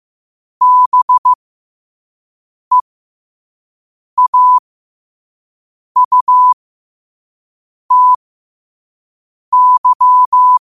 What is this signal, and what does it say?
Morse code, heard as a keyed tone, about 15 wpm, spaced as for 5 wpm, 1000 Hz. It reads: BEAUTY
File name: qotdfiles / 5wpm